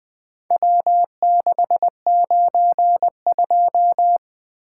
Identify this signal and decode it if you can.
Morse code, heard as a keyed tone, about 23 wpm, 700 Hz. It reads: W692
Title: Speed-Racing - Four Letter-Number Combinations - 20wpm